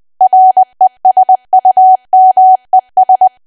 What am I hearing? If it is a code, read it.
Morse code, heard as a keyed tone, about 20 wpm, 750 Hz. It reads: RESUMES
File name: cwa / W1AW-Level3 / 20WPM